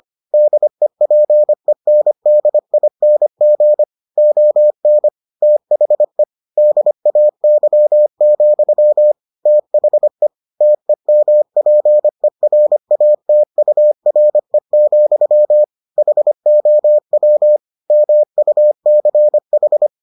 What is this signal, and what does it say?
Morse code, heard as a keyed tone, about 25 wpm, 600 Hz. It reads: DEPENDING ON THE DAY, THE TEMPERATURE, HOW MUCH